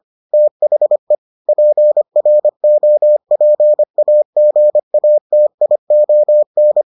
Morse code, heard as a keyed tone, about 25 wpm, 600 Hz. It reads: THE PROPAGATION